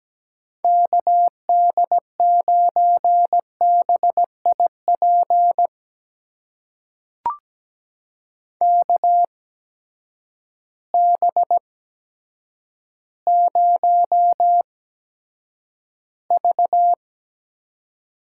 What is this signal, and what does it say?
Morse code, heard as a keyed tone, about 17 wpm, spaced as for 8 wpm, 700 Hz. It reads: KD9BIP KB0V
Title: USA Call Signs - 6 Characters - 8x Character Spacing - 17wpm